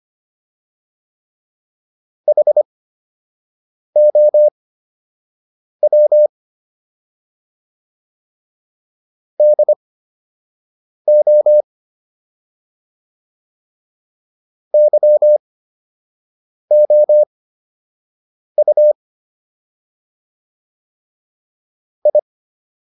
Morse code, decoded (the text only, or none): HOW DO YOU I